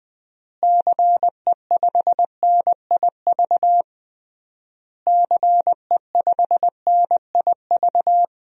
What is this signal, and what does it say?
Morse code, heard as a keyed tone, about 20 wpm, 700 Hz. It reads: CE5NIV CE5NIV
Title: DXCC-100 Call Signs - 6 Characters - Repeated 3x - 20wpm